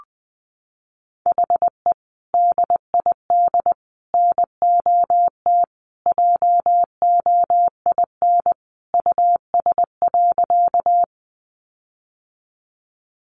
Morse code, heard as a keyed tone, about 19 wpm, 700 Hz. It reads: HE DID NOT JOIN US.
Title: Sentences from Top 1400 Words 20wpm